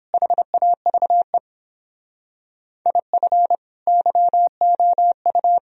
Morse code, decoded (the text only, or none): HAVE IF YOU